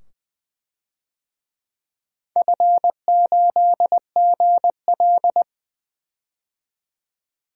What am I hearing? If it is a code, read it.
Morse code, heard as a keyed tone, about 20 wpm, 700 Hz. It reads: F8GL
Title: DXCC-100 Call Signs - 4 Characters - 20wpm